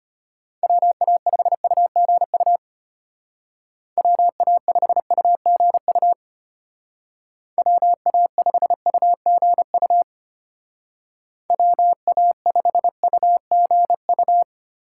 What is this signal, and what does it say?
Morse code, heard as a keyed tone, about 32 wpm, 700 Hz. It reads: WA5UGU WA5UGU WA5UGU WA5UGU